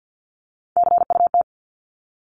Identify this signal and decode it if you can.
Morse code, heard as a keyed tone, about 49 wpm, 700 Hz. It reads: CUT
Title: Top 500 Words 50wpm